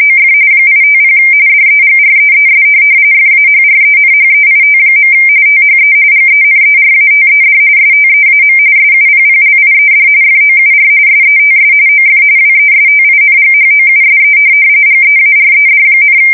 RTTY, about 45.45 baud. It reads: OF 850 DEGREES FAHRENHEIT. THE GENERATED STEAM SUPPLIED FOUR SETS OF WESTINGHOUSE GEARED,
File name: fsk170 / USS-New-Jersey